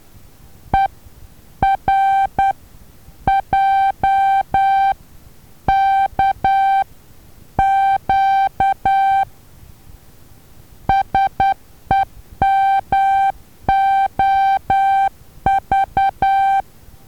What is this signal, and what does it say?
Morse code, heard as a keyed tone, about 10 wpm, spaced as for 8 wpm, 780 Hz. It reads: ERJKQ SEMOV
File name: Taviro HA6VA B